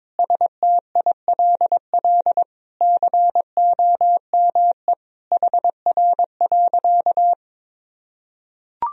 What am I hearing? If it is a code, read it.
Morse code, heard as a keyed tone, about 22 wpm, 700 Hz. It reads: STILL COME HR.
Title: Sentences from Top 300 Words and Abbreviations 22wpm